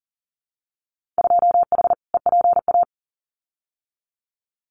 Morse code, heard as a keyed tone, about 40 wpm, 700 Hz. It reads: 2H EPA
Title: Winter Field Day Exchanges 40wpm